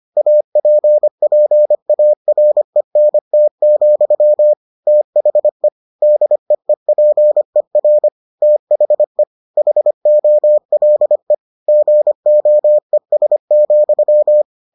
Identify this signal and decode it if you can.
Morse code, heard as a keyed tone, about 25 wpm, 600 Hz. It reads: APPARENT, THE DEEPER THE HOLE GOES,